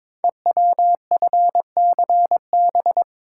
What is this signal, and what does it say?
Morse code, heard as a keyed tone, about 22 wpm, 700 Hz. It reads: EWFCB
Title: Code Groups - 5 Letters - 22wpm